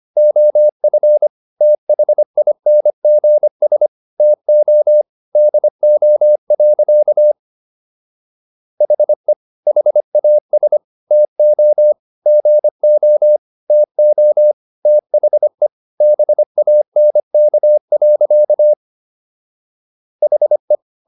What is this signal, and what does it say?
Morse code, heard as a keyed tone, about 25 wpm, 600 Hz. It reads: OF THINGS TO DO. HE HAS TO GO TO THE BANK. HE